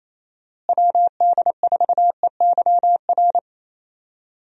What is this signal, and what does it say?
Morse code, heard as a keyed tone, about 27 wpm, 700 Hz. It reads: WD4EYR
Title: USA Call Signs - 6 Characters - Repeated 3x - 28wpm